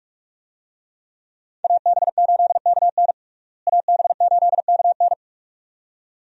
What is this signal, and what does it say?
Morse code, heard as a keyed tone, about 47 wpm, 700 Hz. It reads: AB8KN AB8KN
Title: USA Call Signs - 5 Characters - Repeated 2x - 45wpm